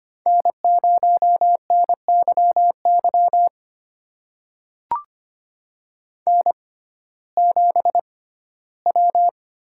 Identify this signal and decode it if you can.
Morse code, heard as a keyed tone, about 25 wpm, spaced as for 15 wpm, 700 Hz. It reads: N0NYY N7W
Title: USA Call Signs - 5 Characters - 6x Character Spacing - 25wpm